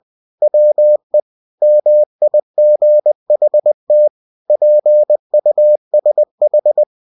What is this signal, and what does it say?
Morse code, heard as a keyed tone, about 20 wpm, 600 Hz. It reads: WE MIGHT PUSH
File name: cw-books / winnie-the-pooh / winnie-20wpm-page-0018